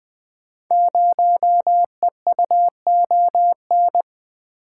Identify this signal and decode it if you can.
Morse code, heard as a keyed tone, about 20 wpm, 700 Hz. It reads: 0EUON
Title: Code Groups - 5 Letter:Numbers - 20wpm